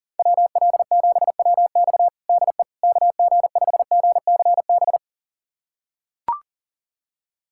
Morse code, heard as a keyed tone, about 40 wpm, 700 Hz. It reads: WL7WX DE KG5GCB
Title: USA Callsign DE Callsign 40wpm